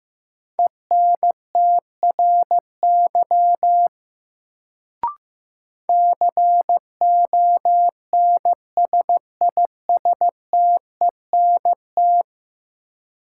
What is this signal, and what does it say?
Morse code, heard as a keyed tone, about 15 wpm, 700 Hz. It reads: ENTRY CONSISTENT
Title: Sentences from Top 1500 Words - New Word Drill Alt - 15wpm